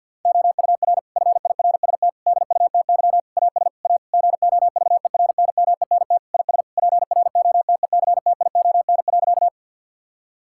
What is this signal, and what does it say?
Morse code, heard as a keyed tone, about 50 wpm, 700 Hz. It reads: OUR FIRST DUTY AS A GOVERNMENT IS PROTECTION.
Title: Sentences from Top 1600 Words 50wpm